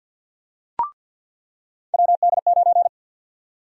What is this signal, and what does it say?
Morse code, heard as a keyed tone, about 48 wpm, 700 Hz. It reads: WD9
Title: USA Call Sign Prefixes - 50wpm